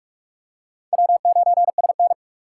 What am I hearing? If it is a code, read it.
Morse code, heard as a keyed tone, about 45 wpm, 700 Hz. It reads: W9SN
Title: USA Call Signs - 4 Characters - 4x Character Spacing - 45wpm